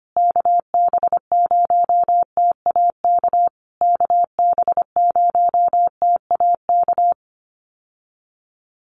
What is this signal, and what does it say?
Morse code, heard as a keyed tone, about 25 wpm, 700 Hz. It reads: KB0TAK KB0TAK
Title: USA Call Signs Repeated 3x 25wpm